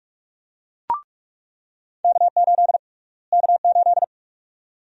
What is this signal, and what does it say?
Morse code, heard as a keyed tone, about 46 wpm, 700 Hz. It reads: K8 K8
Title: USA Call Sign Prefixes - Repeated 2x - 45wpm